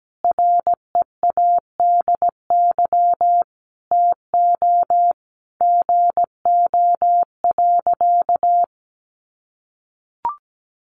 Morse code, heard as a keyed tone, about 17 wpm, 700 Hz. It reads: READY TO GO.
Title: Sentences from Top 700 Words 17wpm